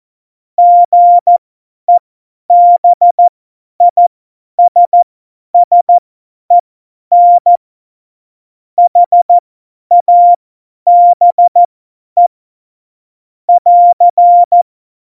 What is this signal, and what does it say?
Morse code, heard as a keyed tone, about 14 wpm, spaced as for 10 wpm, 700 Hz. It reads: GEBISSEN HABE <AR>
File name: Morsekurs / Witze / Witz 26. Dezember 2017, 50 F70